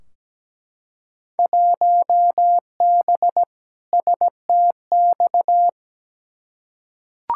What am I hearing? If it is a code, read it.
Morse code, heard as a keyed tone, about 17 wpm, 700 Hz. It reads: 1B STX
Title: Field Day Exchanges - 17wpm